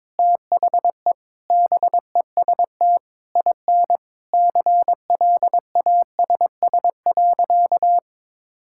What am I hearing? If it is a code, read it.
Morse code, heard as a keyed tone, about 22 wpm, 700 Hz. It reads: THE BEST IN CLASS.
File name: practice / 22wpm / Sentences with 8-Word Phrases - 22wpm